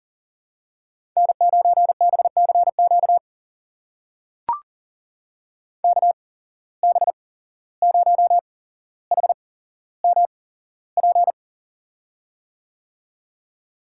Morse code, heard as a keyed tone, about 43 wpm, spaced as for 16 wpm, 700 Hz. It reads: N9BCQ KB0HMP
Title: USA Call Signs - 8x Character Spacing - 40wpm